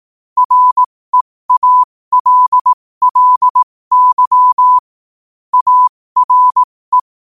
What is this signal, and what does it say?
Morse code, heard as a keyed tone, about 18 wpm, spaced as for 15 wpm, 1000 Hz. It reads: REALLY ARE
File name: qotdfiles / 15wpm